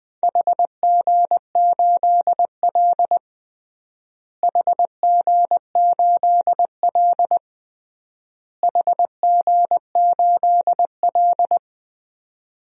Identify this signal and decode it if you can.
Morse code, heard as a keyed tone, about 20 wpm, 690 Hz. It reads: HG8L HG8L HG8L
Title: DXCC-100 Call Signs - 4 Characters - Repeated 3x - 20wpm